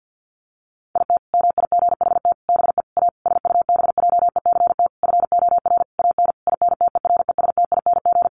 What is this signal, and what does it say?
Morse code, heard as a keyed tone, about 50 wpm, 700 Hz. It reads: IT MIGHT BE A SUBJECT FOR AN INTERESTING
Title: Sentences from Top 1500 Words 50wpm